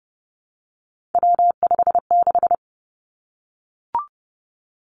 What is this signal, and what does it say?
Morse code, heard as a keyed tone, about 30 wpm, 700 Hz. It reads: W56
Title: Three Challenging Letter-Number Combinations 30wpm